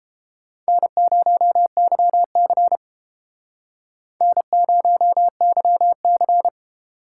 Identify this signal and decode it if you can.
Morse code, heard as a keyed tone, about 29 wpm, 700 Hz. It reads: N0YC N0YC